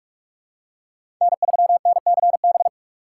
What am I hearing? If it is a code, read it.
Morse code, heard as a keyed tone, about 48 wpm, 700 Hz. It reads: N3NCB
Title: USA Call Signs - 5 Characters - 6x Character Spacing - 45wpm